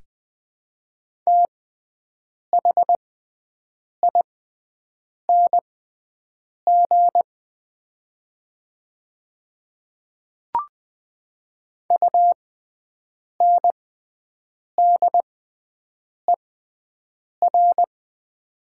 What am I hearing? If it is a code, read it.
Morse code, heard as a keyed tone, about 20 wpm, spaced as for 7 wpm, 700 Hz. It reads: THING UNDER
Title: 6x Character Spacing - Top 300 Words - 20wpm